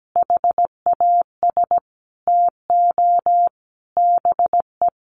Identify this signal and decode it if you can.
Morse code, heard as a keyed tone, about 17 wpm, 700 Hz. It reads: HAS TO BE
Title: Sentences from Top 700 Words 17wpm